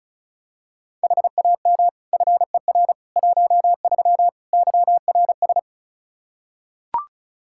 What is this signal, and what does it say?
Morse code, heard as a keyed tone, about 36 wpm, 700 Hz. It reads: HAM FER 13 YRS
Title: K1USN SST QSO Practice 35wpm